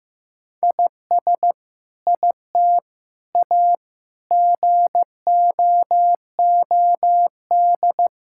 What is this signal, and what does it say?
Morse code, heard as a keyed tone, about 16 wpm, 700 Hz. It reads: IS IT A GOOD
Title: Sentences from Top 1000 Words - Review Part 5 - 15wpm